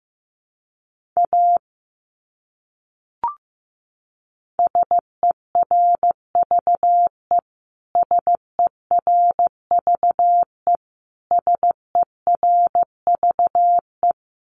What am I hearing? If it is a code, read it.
Morse code, heard as a keyed tone, about 15 wpm, 700 Hz. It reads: A SERVE SERVE SERVE
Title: Top 500 Words Repeated 15wpm